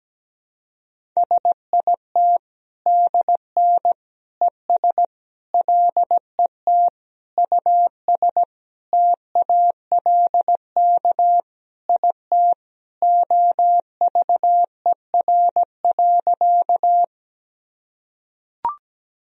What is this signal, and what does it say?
Morse code, heard as a keyed tone, about 17 wpm, 700 Hz. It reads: SIT DN ES LET US TALK IT OVER.
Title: Sentences from Top 500 Words and Abbreviations 17wpm